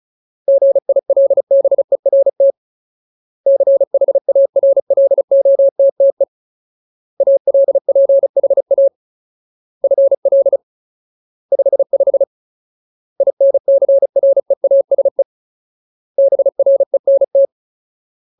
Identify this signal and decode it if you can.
Morse code, heard as a keyed tone, about 34 wpm, 550 Hz. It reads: GILBERT CHARLOTTE ALPHA FL 55 INCREASE BRENT
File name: Ultimate Word List / 35WPM Ultimate